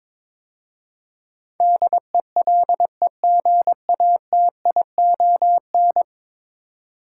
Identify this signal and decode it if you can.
Morse code, heard as a keyed tone, about 22 wpm, 700 Hz. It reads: DELEGATION
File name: practice / 22wpm / 10 Letter Words - First 3 Letters Given - 22wpm